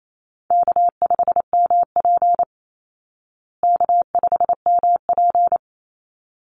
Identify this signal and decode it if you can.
Morse code, heard as a keyed tone, about 28 wpm, 700 Hz. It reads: K5MP K5MP
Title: Call Signs Repeated 3x - 28wpm